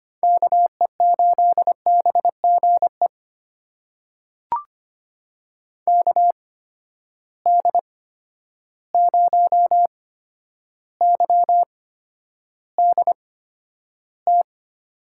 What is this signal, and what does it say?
Morse code, heard as a keyed tone, about 25 wpm, spaced as for 11 wpm, 700 Hz. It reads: KE8BGE KD0YDT
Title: USA Call Signs - 6 Characters - 8x Character Spacing - 25wpm